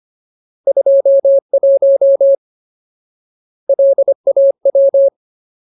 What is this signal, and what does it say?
Morse code, heard as a keyed tone, about 25 wpm, 550 Hz. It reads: 21 LAW
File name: Ultimate Word List / 25WPM Ultimate